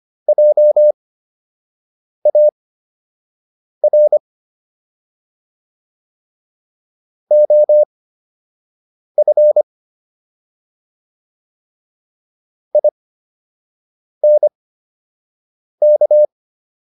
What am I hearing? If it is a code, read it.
Morse code, heard as a keyed tone, about 25 wpm, spaced as for 6 wpm, 600 Hz. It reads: JAR OF INK